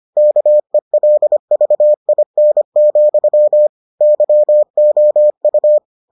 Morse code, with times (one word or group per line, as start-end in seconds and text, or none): 0.2-3.7 KELVIN,
4.0-5.8 YOU